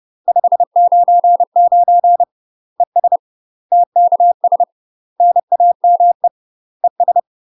Morse code, heard as a keyed tone, about 30 wpm, 700 Hz. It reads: 599 ES TKS NAME ES
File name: UK MASTER QSOs TRAINING / 30WPM / UK MASTER QSOs 0018